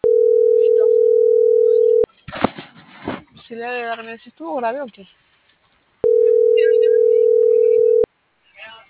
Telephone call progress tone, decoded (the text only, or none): ringback tone